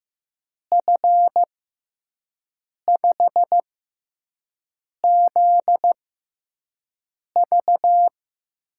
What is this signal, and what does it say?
Morse code, heard as a keyed tone, about 15 wpm, spaced as for 5 wpm, 700 Hz. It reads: F5ZV
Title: USA Call Signs - 6 Characters - 6x Character Spacing - 15wpm